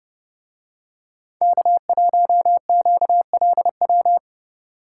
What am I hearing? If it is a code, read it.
Morse code, heard as a keyed tone, about 29 wpm, 700 Hz. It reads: K1QLW